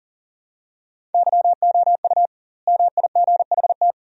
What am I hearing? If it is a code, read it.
Morse code, heard as a keyed tone, about 40 wpm, 700 Hz. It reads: YOU MIGHT
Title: 2x Word-Spacing - Sentences from Top 300 Words - 40wpm